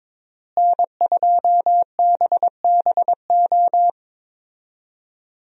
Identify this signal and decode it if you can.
Morse code, heard as a keyed tone, about 22 wpm, 700 Hz. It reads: N2BBO
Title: USA Call Signs - 5 Characters - Repeated 3x - 22wpm